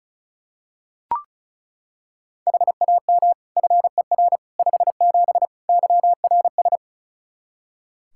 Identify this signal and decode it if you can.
Morse code, heard as a keyed tone, about 36 wpm, 700 Hz. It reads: HAM FER 57 YRS
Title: K1USN SST QSO Practice 35wpm